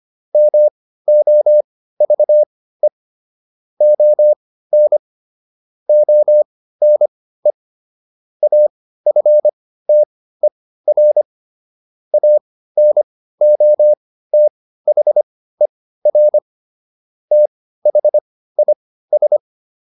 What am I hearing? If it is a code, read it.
Morse code, heard as a keyed tone, about 25 wpm, spaced as for 15 wpm, 600 Hz. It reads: MOVE ON ONE AFTER ANOTHER THIS